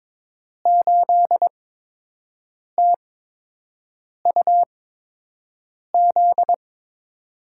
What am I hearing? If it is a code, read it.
Morse code, heard as a keyed tone, about 22 wpm, spaced as for 6 wpm, 700 Hz. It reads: 8TUZ
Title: USA Call Signs - 5 Characters - 8x Character Spacing - 22wpm